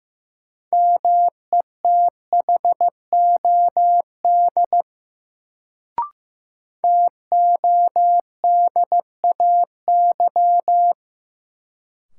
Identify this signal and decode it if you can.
Morse code, heard as a keyed tone, about 15 wpm, 700 Hz. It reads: METHOD TODAY
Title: Sentences from Top 600 Words - New Word Drill - 15wpm